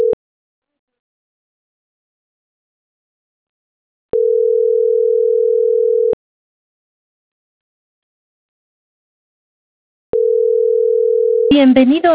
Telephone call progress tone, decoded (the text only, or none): ringback tone